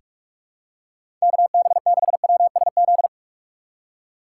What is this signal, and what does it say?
Morse code, heard as a keyed tone, about 42 wpm, 700 Hz. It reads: KB6WSZ
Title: USA Call Signs - 6 Characters - 3x Character Spacing - 45wpm